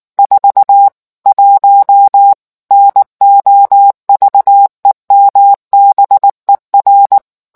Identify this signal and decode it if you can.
Morse code, heard as a keyed tone, about 19 wpm, 800 Hz. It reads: 41 NOVEMBER